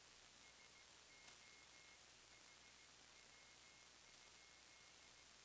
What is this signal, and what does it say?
Morse code, heard as a keyed tone, about 16 wpm, 2150 Hz, with noise. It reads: SO HW?
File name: sdr / WINRAD WB5RVZ PSK Sample 40m